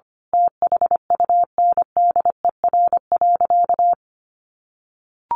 Morse code, heard as a keyed tone, about 25 wpm, 700 Hz. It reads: THUNDER.
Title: A Sound Of Thunder 25wpm